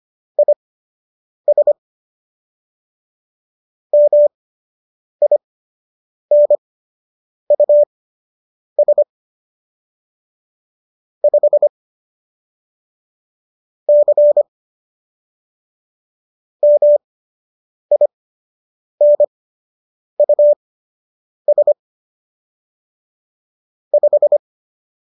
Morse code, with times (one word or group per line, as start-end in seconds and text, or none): 0.4-1.7 IS
3.9-9.0 MINUS
11.2-11.7 5
13.9-14.4 C
16.6-21.7 MINUS
23.9-24.4 5